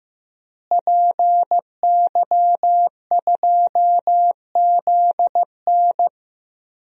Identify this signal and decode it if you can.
Morse code, heard as a keyed tone, about 16 wpm, 700 Hz. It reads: PY2ZN
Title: DXCC-100 Call Signs - 5 Characters - Repeated 3x - 15wpm